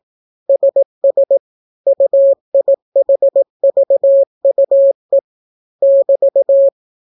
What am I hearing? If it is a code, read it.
Morse code, heard as a keyed tone, about 18 wpm, 550 Hz. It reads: SS UIHVUE <BT>